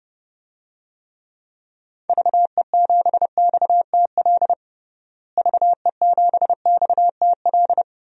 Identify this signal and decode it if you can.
Morse code, heard as a keyed tone, about 30 wpm, 700 Hz. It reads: VE7XTL VE7XTL
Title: DXCC-100 Call Signs - 6 Characters - Repeated 3x - 30wpm